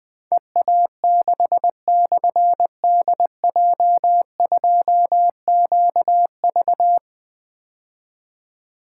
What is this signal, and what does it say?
Morse code, heard as a keyed tone, about 20 wpm, 700 Hz. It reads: EA6/DJ2QV
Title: Difficult Call Signs 20wpm